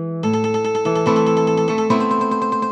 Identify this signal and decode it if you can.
There is no code.